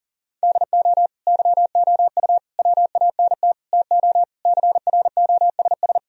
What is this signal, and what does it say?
Morse code, heard as a keyed tone, about 40 wpm, 700 Hz. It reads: DO YOU WANT TO CROSS
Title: Sentences from Top 1000 Words - Review Part 1 - 40wpm